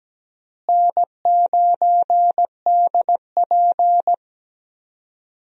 Morse code, heard as a keyed tone, about 17 wpm, 700 Hz. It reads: N9DP
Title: USA Call Signs - 4 Characters - Repeated 3x - 17wpm